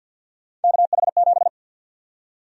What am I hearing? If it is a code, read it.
Morse code, heard as a keyed tone, about 53 wpm, 700 Hz. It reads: KH7
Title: USA Call Sign Prefixes - 50wpm